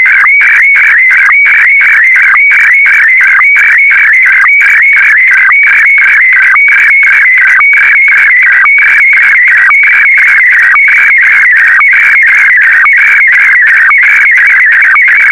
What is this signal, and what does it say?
SSTV (slow-scan television), Scottie DX.
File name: SDX-VK4MSL